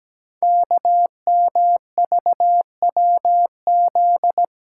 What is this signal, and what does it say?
Morse code, heard as a keyed tone, about 17 wpm, 700 Hz. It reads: KMVWZ
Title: Code Groups - 5 Letters - 17wpm